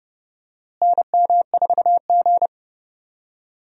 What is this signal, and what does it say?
Morse code, heard as a keyed tone, about 30 wpm, 700 Hz. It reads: NM4G
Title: USA Call Signs - 4 Characters - Repeated 2x - 30wpm